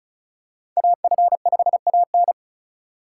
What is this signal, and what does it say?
Morse code, heard as a keyed tone, about 36 wpm, 700 Hz. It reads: AF5AN
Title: USA Call Signs - 5 Characters - 2x Character Spacing - 35wpm